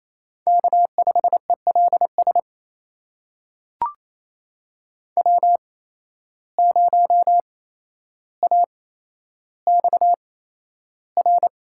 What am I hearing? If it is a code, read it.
Morse code, heard as a keyed tone, about 28 wpm, spaced as for 12 wpm, 700 Hz. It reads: K5ELS W0AXR
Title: USA Call Signs - 5 Characters - 8x Character Spacing - 28wpm